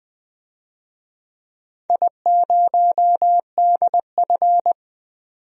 Morse code, heard as a keyed tone, about 20 wpm, 700 Hz. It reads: I0DF
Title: DXCC-100 Call Signs - 4 Characters - 20wpm